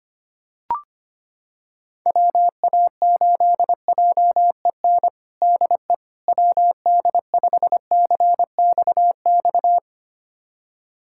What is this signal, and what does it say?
Morse code, heard as a keyed tone, about 25 wpm, 700 Hz. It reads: WA8JEN DE WD5CXX
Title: USA Callsign DE Callsign 25wpm